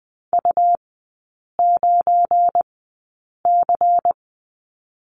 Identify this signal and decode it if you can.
Morse code, heard as a keyed tone, about 20 wpm, spaced as for 8 wpm, 700 Hz. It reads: U9C
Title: 191 Lesson 31e - Letter v - Three Characters 20wpm